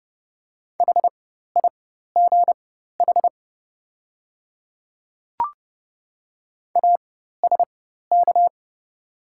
Morse code, heard as a keyed tone, about 30 wpm, spaced as for 14 wpm, 700 Hz. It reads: HIGH ASK
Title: 4x Character Spacing - Top 200 Words - 30wpm